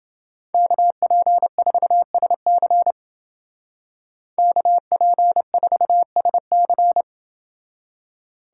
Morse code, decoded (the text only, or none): KP4SC KP4SC